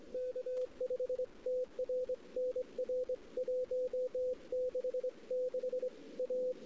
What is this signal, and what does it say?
Morse code, heard as a keyed tone, about 22 wpm, 510 Hz. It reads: K5TR NR 166 A
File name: CW / 2008-ss-cw-k5tr-op-n5rz -14:00